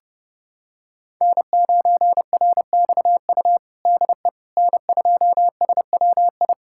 Morse code, heard as a keyed tone, about 31 wpm, 700 Hz. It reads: N9RXU DE N2SWI
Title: USA Callsign DE Callsign - 5 Characters - 30wpm